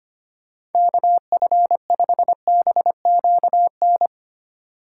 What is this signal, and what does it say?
Morse code, heard as a keyed tone, about 32 wpm, 700 Hz. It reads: KF5BQN